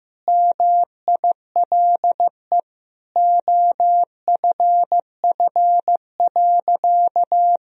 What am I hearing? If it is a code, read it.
Morse code, heard as a keyed tone, about 15 wpm, 700 Hz. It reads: MILE OFF.
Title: Sentences from Top 1000 Words - Review Part 1 - 15wpm